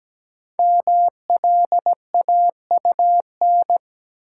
Morse code, heard as a keyed tone, about 17 wpm, 700 Hz. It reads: MLAUN
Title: Code Groups - 5 Letters - 17wpm